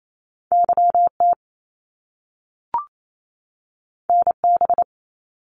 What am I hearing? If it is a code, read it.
Morse code, heard as a keyed tone, about 28 wpm, 700 Hz. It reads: YT NB
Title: Canadian Provinces and Territories 28wpm